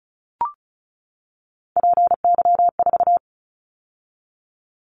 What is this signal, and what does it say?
Morse code, heard as a keyed tone, about 34 wpm, 700 Hz. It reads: PY4
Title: Callsign Prefixes for Top 50 DXCC - 35wpm